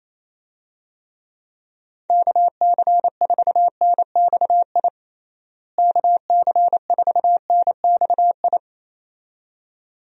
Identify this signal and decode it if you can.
Morse code, heard as a keyed tone, about 28 wpm, 700 Hz. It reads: KC4NXI KC4NXI
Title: USA Call Signs - 6 Characters - Repeated 2x - 28wpm